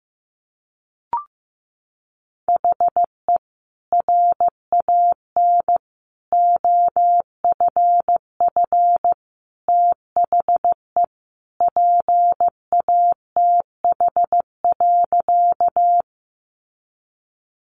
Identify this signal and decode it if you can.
Morse code, heard as a keyed tone, about 15 wpm, 700 Hz. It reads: HE RAN OFF THE PATH.
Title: A Sound Of Thunder 15wpm